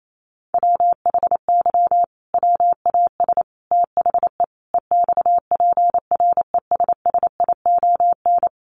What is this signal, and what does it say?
Morse code, heard as a keyed tone, about 28 wpm, 700 Hz. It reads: WHY WAS THE EXPRESSION
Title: Sentences from Top 1400 Words 28wpm